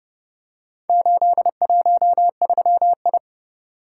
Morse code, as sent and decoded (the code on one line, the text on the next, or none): ---.. .---- ...-- ..
813I